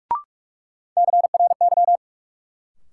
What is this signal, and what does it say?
Morse code, heard as a keyed tone, about 41 wpm, 700 Hz. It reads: CRY
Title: Sets of 5 Words Encore - New Word Drill - 45wpm